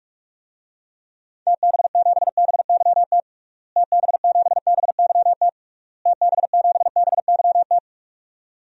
Morse code, heard as a keyed tone, about 46 wpm, 700 Hz. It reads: TB7BYT TB7BYT TB7BYT
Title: DXCC-100 Call Signs - 6 Characters - Repeated 3x - 45wpm